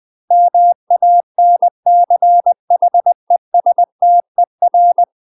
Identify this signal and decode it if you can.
Morse code, heard as a keyed tone, about 20 wpm, 700 Hz. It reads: MANCHESTER
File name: UK MASTER QSOs TRAINING / 20WPM / UK MASTER QSOs 0040